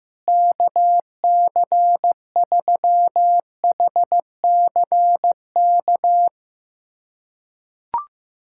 Morse code, heard as a keyed tone, about 15 wpm, 700 Hz. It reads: KC3HCK
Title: USA Callsign DE Callsign - 6 Characters - 15wpm